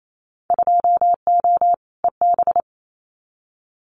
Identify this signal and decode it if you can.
Morse code, heard as a keyed tone, about 28 wpm, 690 Hz. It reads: 2O EB
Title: Winter Field Day Exchanges Encore 28wpm